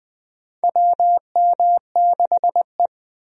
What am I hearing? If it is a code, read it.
Morse code, heard as a keyed tone, about 20 wpm, 700 Hz. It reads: WM6E
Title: USA Call Signs - 4 Characters - Repeated 2x - 20wpm